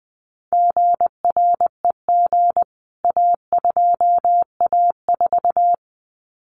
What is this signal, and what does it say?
Morse code, heard as a keyed tone, about 20 wpm, 700 Hz. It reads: GREG A2A4
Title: CWT Exchange 20wpm